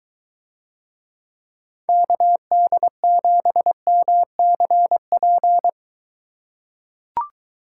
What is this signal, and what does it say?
Morse code, heard as a keyed tone, about 23 wpm, 700 Hz. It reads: KD7MCP